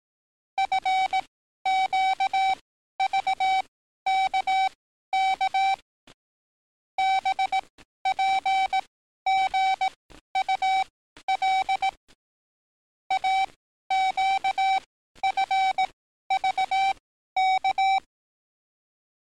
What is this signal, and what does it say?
Morse code, heard as a keyed tone, about 18 wpm, spaced as for 12 wpm, 770 Hz. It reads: FQVKK BPGUL AQFVK
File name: training / NAVPERS 11049to11060 / NAVPERS 11052 RA RB